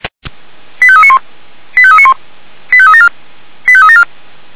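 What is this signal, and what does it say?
Five-tone selective call (ZVEI), heard as quick runs of five tones, tones of about 70 ms. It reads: 85391 85391 85374 85374
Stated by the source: ZVEI five-tone selective call received by SDR, Feuerwehr Orlamünde.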